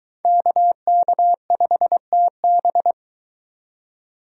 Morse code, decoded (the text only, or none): KK5TB